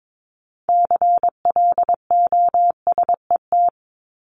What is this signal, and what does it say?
Morse code, heard as a keyed tone, about 22 wpm, 700 Hz. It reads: CLOSET
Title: Top 1751-2000 Words 22wpm